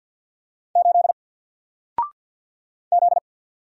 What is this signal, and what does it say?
Morse code, heard as a keyed tone, about 48 wpm, 700 Hz. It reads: 8 Z
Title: Single Letter-Number Alt 50wpm